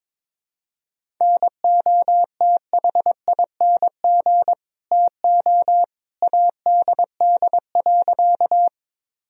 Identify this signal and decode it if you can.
Morse code, heard as a keyed tone, about 22 wpm, 700 Hz. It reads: NOTHING TO ADD.
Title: Sentences from Top 1000 Words - Review Part 1 - 22wpm